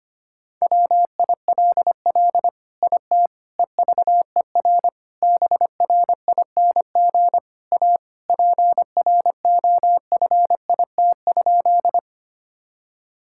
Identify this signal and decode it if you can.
Morse code, heard as a keyed tone, about 25 wpm, 700 Hz. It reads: WILL IT EVER BRING A PROFIT?
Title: Sentences from Top 1000 Words - Review Part 5 - 25wpm